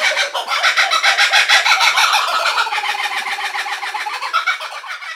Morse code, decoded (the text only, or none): none